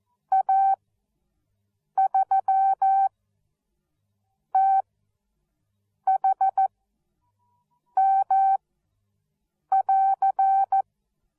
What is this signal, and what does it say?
Morse code, heard as a keyed tone, about 14 wpm, spaced as for 6 wpm, 770 Hz. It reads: A3THM<AR>